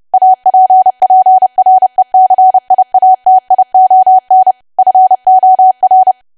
Morse code, heard as a keyed tone, about 30 wpm, 750 Hz. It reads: APPRECIATION FOR
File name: cwa / W1AW-Level3 / 30WPM